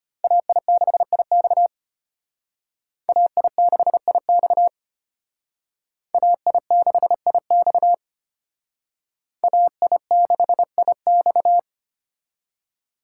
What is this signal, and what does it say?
Morse code, heard as a keyed tone, about 32 wpm, 700 Hz. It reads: AI6IX AI6IX AI6IX AI6IX